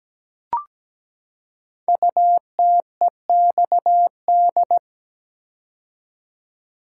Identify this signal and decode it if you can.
Morse code, heard as a keyed tone, about 17 wpm, 700 Hz. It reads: UTEXD
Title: Code Groups - 5 Letters - 17wpm